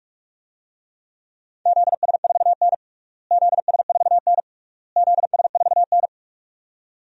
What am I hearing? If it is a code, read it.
Morse code, heard as a keyed tone, about 46 wpm, 700 Hz. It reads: ZS4N ZS4N ZS4N
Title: DXCC-100 Call Signs - 4 Characters - Repeated 3x - 45wpm